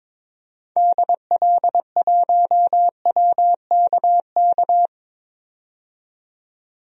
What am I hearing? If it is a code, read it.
Morse code, heard as a keyed tone, about 21 wpm, 700 Hz. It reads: DL1WKK
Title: DXCC-100 Call Signs - 6 Characters - 22wpm